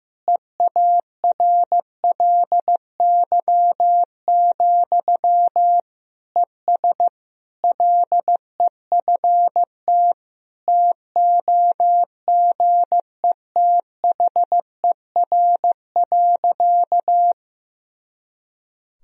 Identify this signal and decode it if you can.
Morse code, heard as a keyed tone, about 16 wpm, 700 Hz. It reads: EARLY, ES LEFT TOGETHER.
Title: Sentences from Top 400 Words and Abbreviations 15wpm